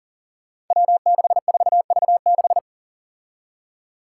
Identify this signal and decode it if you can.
Morse code, heard as a keyed tone, about 42 wpm, 700 Hz. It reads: W64V6
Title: Five Challenging Letter-Number Combinations 40wpm